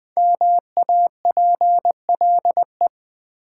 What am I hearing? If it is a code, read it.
Morse code, heard as a keyed tone, about 20 wpm, 700 Hz. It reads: MAPLE
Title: Easy Words with Hints - 20wpm